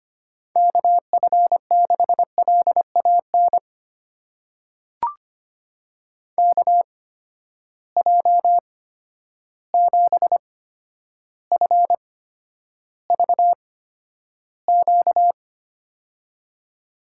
Morse code, heard as a keyed tone, about 25 wpm, spaced as for 11 wpm, 700 Hz. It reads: KF6LAN KJ7FVQ
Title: USA Call Signs - 6 Characters - 8x Character Spacing - 25wpm